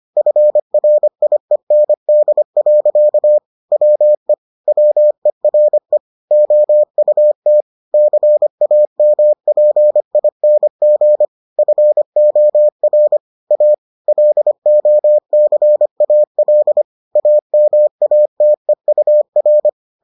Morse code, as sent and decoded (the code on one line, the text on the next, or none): ..-. .-. .. -. -.. .-.-.- / .-- . / .-- . .-. / --- ..- - / -.-. .- -- .--. .. -. --. / ..-. --- .-. / .- / .-.. --- -.-. .- .-.. / .- -- .- - . ..- .-.
FRIEND. WE WERE OUT CAMPING FOR A LOCAL AMATEUR